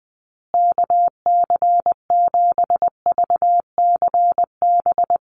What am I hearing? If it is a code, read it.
Morse code, heard as a keyed tone, about 20 wpm, 700 Hz. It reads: KC7VCB
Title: USA Call Signs Repeated 2x 20wpm